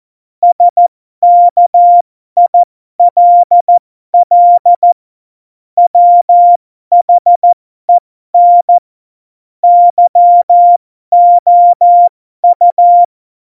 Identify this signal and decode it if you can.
Morse code, heard as a keyed tone, about 14 wpm, spaced as for 12 wpm, 700 Hz. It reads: SKILL WHEN YOU
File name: Morsekurs / Witze